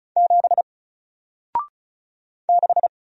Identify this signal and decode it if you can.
Morse code, heard as a keyed tone, about 36 wpm, 700 Hz. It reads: 7 6
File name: Single-Letter-Number-35wpm